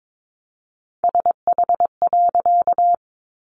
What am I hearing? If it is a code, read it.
Morse code, heard as a keyed tone, about 22 wpm, 700 Hz. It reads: SH.
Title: A Sound Of Thunder 22wpm